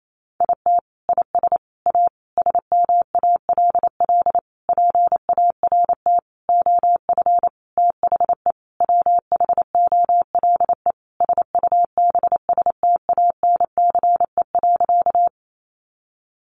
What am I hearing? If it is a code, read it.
Morse code, heard as a keyed tone, about 28 wpm, 700 Hz. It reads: IT IS A SMALL PART OF THE WHOLE SUBSTANCE.
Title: Sentences from Top 1500 Words 28wpm